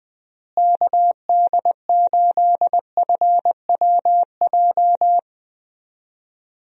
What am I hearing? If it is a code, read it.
Morse code, heard as a keyed tone, about 21 wpm, 700 Hz. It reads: KD8FWJ
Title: USA Call Signs - 6 Characters - Repeated 3x - 20wpm